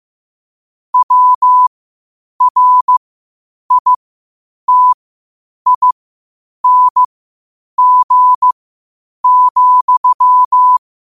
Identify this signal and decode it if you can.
Morse code, heard as a keyed tone, about 15 wpm, spaced as for 8 wpm, 1000 Hz. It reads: WRITING,
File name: qotdfiles / 8wpm